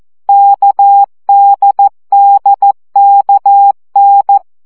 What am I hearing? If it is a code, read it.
Morse code, heard as a keyed tone, about 14 wpm, 800 Hz. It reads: KDDKN
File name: CW gyakorlas n d k 12 baud